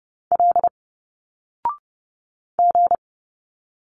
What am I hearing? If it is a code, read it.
Morse code, heard as a keyed tone, about 30 wpm, 700 Hz. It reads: L G
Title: Single Letters 30wpm